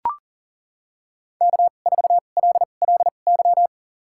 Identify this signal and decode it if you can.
Morse code, heard as a keyed tone, about 40 wpm, spaced as for 29 wpm, 700 Hz. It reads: K4LLY
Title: USA Call Signs - 5 Characters - 2x Character Spacing - 40wpm